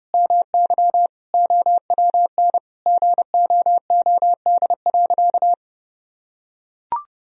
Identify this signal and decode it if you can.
Morse code, heard as a keyed tone, about 30 wpm, 700 Hz. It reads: MY OWN GOOD.
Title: Sentences from Top 1000 Words - Review Part 4 - 30wpm